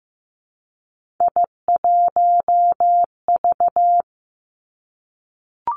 Morse code, heard as a keyed tone, about 15 wpm, 700 Hz. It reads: I1V
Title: Three Challenging Letter-Number Combinations 15wpm